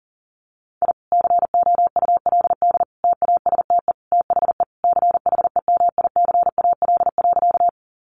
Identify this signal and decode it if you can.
Morse code, heard as a keyed tone, about 40 wpm, 700 Hz. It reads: I COULD TASTE THE CHEMICAL.
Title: Sentences from Top 1200 Words 40wpm